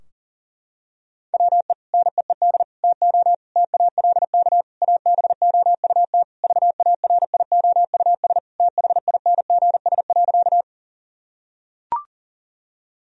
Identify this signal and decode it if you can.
Morse code, heard as a keyed tone, about 40 wpm, 700 Hz. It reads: WE NEED TO TALK ABOUT VARIOUS THINGS.
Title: Sentences from Top 1000 Words - Review Part 4 - 40wpm